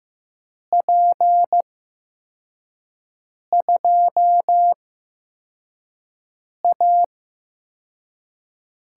Morse code, heard as a keyed tone, about 15 wpm, spaced as for 4 wpm, 700 Hz. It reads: P2A
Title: USA Call Signs - 6 Characters - 8x Character Spacing - 15wpm